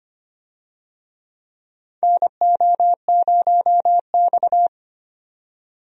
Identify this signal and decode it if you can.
Morse code, heard as a keyed tone, about 25 wpm, 700 Hz. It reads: NO0X
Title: USA Call Signs - 4 Characters - 25wpm